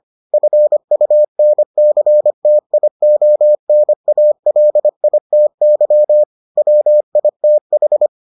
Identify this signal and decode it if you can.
Morse code, heard as a keyed tone, about 25 wpm, 600 Hz. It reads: FUNCTIONALITY WITH